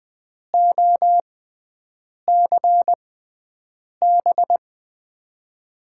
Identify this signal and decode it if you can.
Morse code, heard as a keyed tone, about 20 wpm, spaced as for 7 wpm, 700 Hz. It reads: OCB
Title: USA Call Signs - 6x Character Spacing - 20wpm